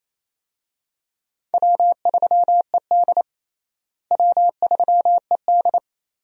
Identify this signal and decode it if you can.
Morse code, heard as a keyed tone, about 28 wpm, 700 Hz. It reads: W3ED W3ED
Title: USA Call Signs - 4 Characters - Repeated 2x - 28wpm